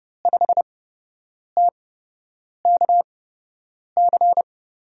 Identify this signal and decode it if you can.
Morse code, heard as a keyed tone, about 30 wpm, spaced as for 8 wpm, 700 Hz. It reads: 5TKC
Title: USA Call Signs - 5 Characters - 8x Character Spacing - 30wpm